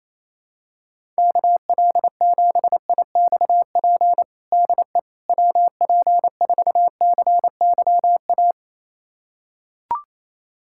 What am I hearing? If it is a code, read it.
Morse code, heard as a keyed tone, about 28 wpm, 700 Hz. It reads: KL7IXP DE WP4CYA
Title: USA Callsign DE Callsign - 6 Characters - 28wpm